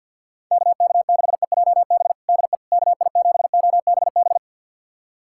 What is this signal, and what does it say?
Morse code, heard as a keyed tone, about 48 wpm, 700 Hz. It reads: KK6EJB DE KI7OBB
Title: USA Callsign DE Callsign - 6 Characters - 50wpm